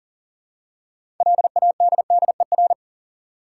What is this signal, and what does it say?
Morse code, heard as a keyed tone, about 40 wpm, 700 Hz. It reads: LADDER
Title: Six Letter Words - 40wpm